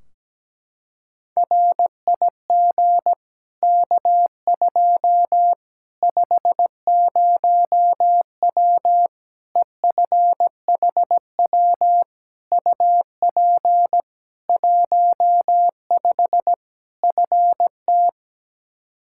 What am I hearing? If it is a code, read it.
Morse code, heard as a keyed tone, about 17 wpm, 700 Hz. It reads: RIG K2 50W EFHW UP 15 FT
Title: Practice QSO - Exchange 2 - Info - 17wpm